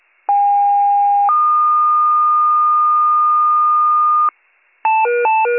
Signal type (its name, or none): two-tone page